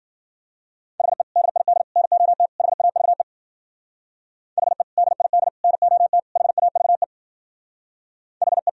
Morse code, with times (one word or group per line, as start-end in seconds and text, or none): 1.0-1.2 HE
1.4-1.8 DID
2.0-2.5 NOT
2.6-3.2 HAVE
4.6-4.8 HE
5.0-5.5 DID
5.6-6.2 NOT
6.4-7.0 HAVE
8.4-8.7 HE